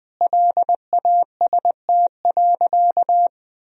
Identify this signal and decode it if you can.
Morse code, heard as a keyed tone, about 20 wpm, 700 Hz. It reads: LAST.
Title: Sentences from Top 300 Words and Abbreviations 20wpm